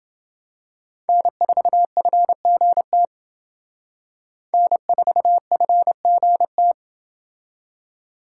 Morse code, heard as a keyed tone, about 29 wpm, 700 Hz. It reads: N4FGT N4FGT